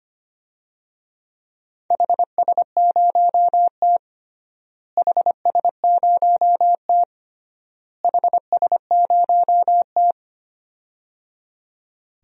Morse code, heard as a keyed tone, about 25 wpm, 700 Hz. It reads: HS0T HS0T HS0T
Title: DXCC-100 Call Signs - 4 Characters - Repeated 3x - 25wpm